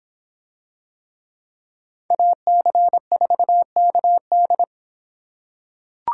Morse code, heard as a keyed tone, about 24 wpm, 700 Hz. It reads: AC4KD